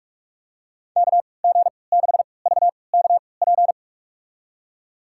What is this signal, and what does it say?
Morse code, heard as a keyed tone, about 46 wpm, spaced as for 26 wpm, 700 Hz. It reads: KG6VKP
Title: USA Call Signs - 3x Character Spacing - 45wpm